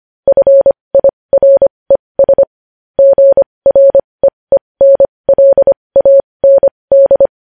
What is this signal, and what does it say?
Morse code, heard as a keyed tone, about 25 wpm, spaced as for 20 wpm, 550 Hz. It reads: FIRES GREENLAND
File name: podcast / News headlines-1x-25x20